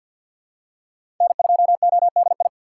Morse code, heard as a keyed tone, about 52 wpm, 700 Hz. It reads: N2ODI
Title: USA Call Signs - 5 Characters - 3x Character Spacing - 50wpm